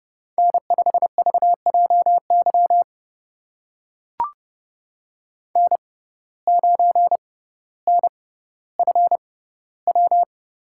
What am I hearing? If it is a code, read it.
Morse code, heard as a keyed tone, about 30 wpm, spaced as for 15 wpm, 700 Hz. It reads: N5VJY N9NFW
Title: USA Call Signs - 5 Characters - 6x Character Spacing - 30wpm